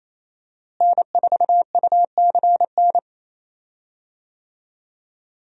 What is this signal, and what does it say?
Morse code, heard as a keyed tone, about 28 wpm, 700 Hz. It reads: N4UCN
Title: USA Call Signs - 5 Characters - Repeated 3x - 28wpm